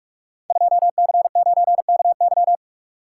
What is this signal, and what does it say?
Morse code, heard as a keyed tone, about 44 wpm, 700 Hz. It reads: 2C9KY
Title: Code Groups - 5 Letter:Numbers - 45wpm